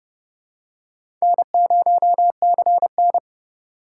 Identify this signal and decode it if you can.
Morse code, heard as a keyed tone, about 27 wpm, 700 Hz. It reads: N0CN